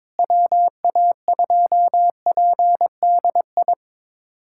Morse code, heard as a keyed tone, about 22 wpm, 700 Hz. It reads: WA2PDI
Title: USA Callsign DE Callsign - 6 Characters - 22wpm